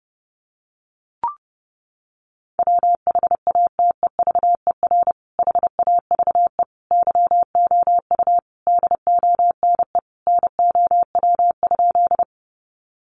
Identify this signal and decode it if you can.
Morse code, heard as a keyed tone, about 30 wpm, 700 Hz. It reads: WHATEVER HAVE YOU DONE NOW?
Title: Sentences from Top 1100 Words 30wpm